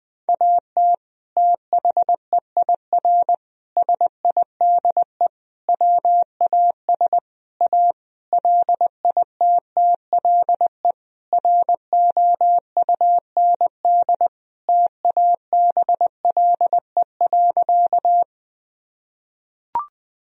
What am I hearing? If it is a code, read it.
Morse code, heard as a keyed tone, about 20 wpm, 700 Hz. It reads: AT THEIR SIDE WAS A LITTLE ROUND TABLE.
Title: Sentences from Top 1000 Words - Review Part 2 - 20wpm